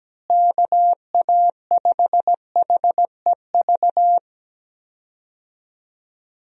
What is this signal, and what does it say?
Morse code, heard as a keyed tone, about 17 wpm, 700 Hz. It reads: KA5HEV
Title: USA Call Signs - 6 Characters - Repeated 3x - 17wpm